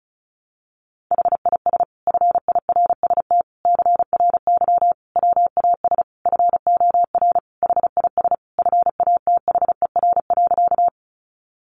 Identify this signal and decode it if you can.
Morse code, heard as a keyed tone, about 34 wpm, 700 Hz. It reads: HIS FIRST CRY WAS FOR HIS FATHER.
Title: Sentences from Top 500 Words 35wpm